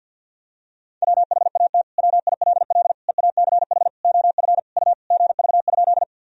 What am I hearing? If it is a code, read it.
Morse code, heard as a keyed tone, about 50 wpm, 700 Hz. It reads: WHAT WILL EACH OF U GV?
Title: Sentences from Top 200 Words and Abbreviations 50wpm